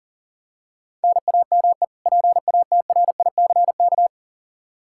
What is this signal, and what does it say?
Morse code, heard as a keyed tone, about 37 wpm, 700 Hz. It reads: NAME PATRICK
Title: Practice QSO - Exchange 1 - Beginning - 40wpm